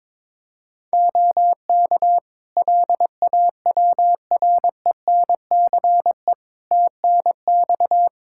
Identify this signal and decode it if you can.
Morse code, heard as a keyed tone, about 22 wpm, 700 Hz. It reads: OK LAWRENCE TNX
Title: Practice QSO - Exchange 4 - Ending - 22wpm